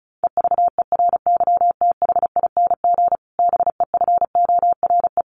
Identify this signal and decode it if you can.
Morse code, heard as a keyed tone, about 34 wpm, 700 Hz. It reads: EVERYTHING BEFORE
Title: Sentences from Top 1500 Words 35wpm